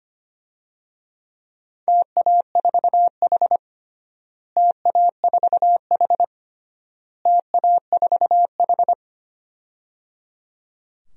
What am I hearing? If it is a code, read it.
Morse code, heard as a keyed tone, about 25 wpm, 700 Hz. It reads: TA4H TA4H TA4H
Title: DXCC-100 Call Signs - 4 Characters - Repeated 3x - 25wpm